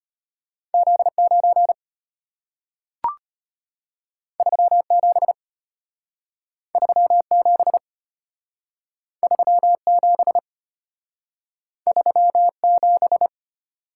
Morse code, heard as a keyed tone, about 34 wpm, 700 Hz. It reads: Z9 37 37 37 37